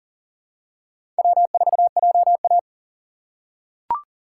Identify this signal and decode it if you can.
Morse code, heard as a keyed tone, about 40 wpm, 700 Hz. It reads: W4JA
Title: USA Call Signs - 4 Characters - 4x Character Spacing - 40wpm